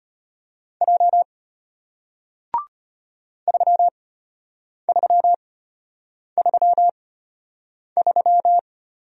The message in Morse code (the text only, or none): J 3333